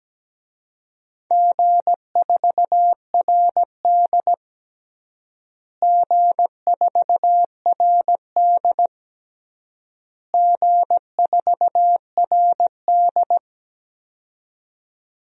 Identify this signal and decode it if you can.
Morse code, heard as a keyed tone, about 17 wpm, 700 Hz. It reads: G4RD G4RD G4RD
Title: DXCC-100 Call Signs - 4 Characters - Repeated 3x - 17wpm